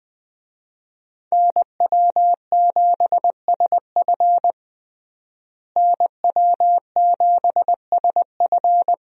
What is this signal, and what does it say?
Morse code, heard as a keyed tone, about 21 wpm, 700 Hz. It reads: NW7SF NW7SF
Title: USA Call Signs - 5 Characters - Repeated 3x - 20wpm